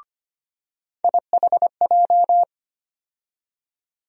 Morse code, heard as a keyed tone, about 25 wpm, 700 Hz. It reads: IHJ
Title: USA Call Sign Postfixes - 3 Letters 25wpm